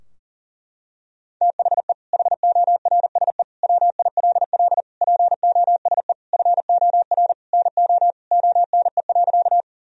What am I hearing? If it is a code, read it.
Morse code, heard as a keyed tone, about 40 wpm, 700 Hz. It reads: THE HORSE WILL POSE FOR NO ONE.
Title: Sentences from Top 1000 Words - Review Part 2 - 40wpm